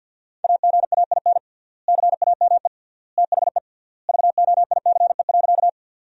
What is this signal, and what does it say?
Morse code, heard as a keyed tone, about 53 wpm, 700 Hz. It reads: AGAIN CAME THE VOICE.
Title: 3x Word-Spacing - Sentences from Top 500 Words - 50wpm